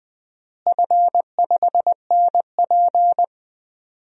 Morse code, heard as a keyed tone, about 20 wpm, 700 Hz. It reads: F5NP
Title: DXCC-100 Call Signs - 4 Characters - Repeated 3x - 20wpm